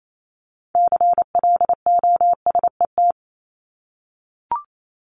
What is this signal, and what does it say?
Morse code, heard as a keyed tone, about 28 wpm, 700 Hz. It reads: CLOSET
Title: Top 1751-2000 Words 28wpm